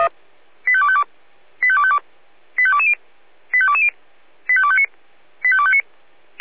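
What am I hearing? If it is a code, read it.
Five-tone selective call (ZVEI), heard as quick runs of five tones, tones of about 70 ms. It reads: 85252 85252 85220 85220 85259 85259
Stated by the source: Five-tone ZVEI selective call received on SDR.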